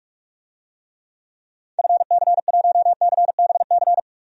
Morse code, heard as a keyed tone, about 46 wpm, 700 Hz. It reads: FC1CBC
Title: DXCC-100 Call Signs - 6 Characters - 45wpm